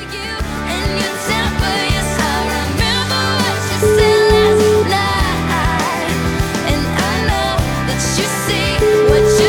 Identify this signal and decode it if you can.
Ringback tone (Europe, 425 Hz), with noise.